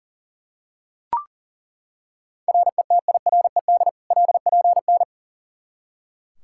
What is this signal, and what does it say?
Morse code, heard as a keyed tone, about 37 wpm, 700 Hz. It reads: RETIRED LPN
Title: Practice QSO - Exchange 3 - More Info - 40wpm